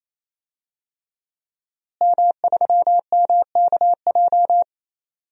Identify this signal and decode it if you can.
Morse code, heard as a keyed tone, about 28 wpm, 700 Hz. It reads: M3MKJ
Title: DXCC-100 Call Signs - 5 Characters - Repeated 3x - 28wpm